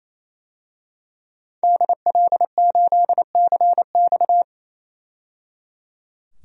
Morse code, heard as a keyed tone, about 27 wpm, 700 Hz. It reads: DL8CX
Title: DXCC-100 Call Signs - 5 Characters - 28wpm